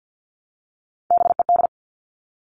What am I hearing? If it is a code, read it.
Morse code, heard as a keyed tone, about 50 wpm, 700 Hz. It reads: BED
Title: Top 500 Words 50wpm